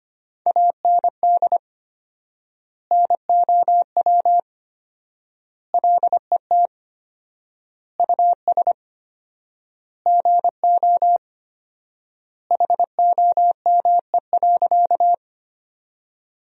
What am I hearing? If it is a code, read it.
Morse code, heard as a keyed tone, about 25 wpm, 700 Hz. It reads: AND NOW LET US GO HOME.
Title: 4x Word-Spacing - Sentences from Top 300 Words - 25wpm